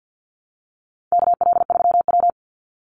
Morse code, heard as a keyed tone, about 50 wpm, 700 Hz. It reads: KL3W
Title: Call Signs Repeated 3x - 50wpm